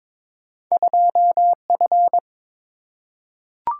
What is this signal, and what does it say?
Morse code, heard as a keyed tone, about 22 wpm, 700 Hz. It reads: 2F
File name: practice / 22wpm / Two Letter-Number Combinations - 22wpm